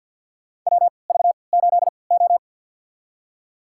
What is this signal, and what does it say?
Morse code, heard as a keyed tone, about 51 wpm, spaced as for 28 wpm, 700 Hz. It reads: WV8O
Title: USA Call Signs - 4 Characters - 3x Character Spacing - 50wpm